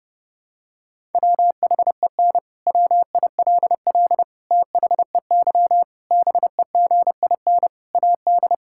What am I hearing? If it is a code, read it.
Morse code, heard as a keyed tone, about 31 wpm, 700 Hz. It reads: WHEN WILL THEY BEGIN AD
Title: Sentences from Top 1000 Words - Review Part 6 - 30wpm